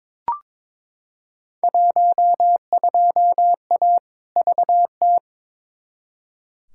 Morse code, heard as a keyed tone, about 22 wpm, 700 Hz. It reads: 12A VT
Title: Field Day Exchanges Encore - 22wpm